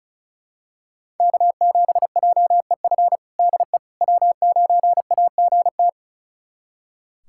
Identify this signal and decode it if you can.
Morse code, heard as a keyed tone, about 36 wpm, 700 Hz. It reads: K7JEF DE W9AGT
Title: USA Callsign DE Callsign - 5 Characters - 35wpm